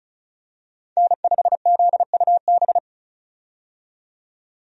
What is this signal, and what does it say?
Morse code, heard as a keyed tone, about 36 wpm, 700 Hz. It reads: N5ZUB
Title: USA Call Signs - 5 Characters - 35wpm